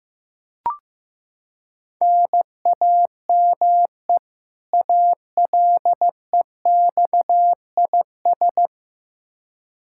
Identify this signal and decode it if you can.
Morse code, heard as a keyed tone, about 15 wpm, 700 Hz. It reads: NAME ALEXIS
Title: Practice QSO - Exchange 1 - Beginning - 15wpm